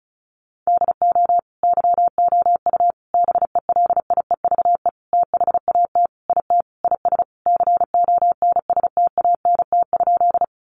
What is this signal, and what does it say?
Morse code, heard as a keyed tone, about 34 wpm, 700 Hz. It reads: DO YOU BELIEVE THAT IT IS CONSTANT?
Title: Sentences from Top 1500 Words 35wpm